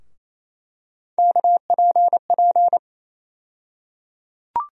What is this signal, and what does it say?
Morse code, heard as a keyed tone, about 28 wpm, 700 Hz. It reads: KPP
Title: USA Call Sign Postfixes - 3 Letters 28wpm